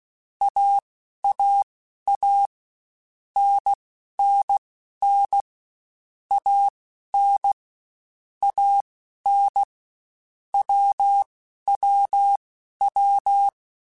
Morse code, heard as a keyed tone, about 16 wpm, spaced as for 11 wpm, 800 Hz. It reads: AAA NNN AN AN WWW